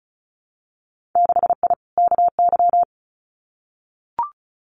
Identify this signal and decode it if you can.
Morse code, heard as a keyed tone, about 36 wpm, 700 Hz. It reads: 6I KY
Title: Winter Field Day Exchanges Encore 35wpm